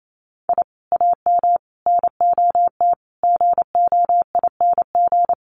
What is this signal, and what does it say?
Morse code, heard as a keyed tone, about 28 wpm, 700 Hz. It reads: I AM NOT GOING
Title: Sets of 5 Words Encore 28wpm